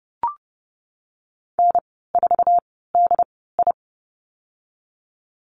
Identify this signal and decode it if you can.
Morse code, heard as a keyed tone, about 30 wpm, spaced as for 17 wpm, 700 Hz. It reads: N4DI
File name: Lesson 19g - Letter w - Call Signs 30wpm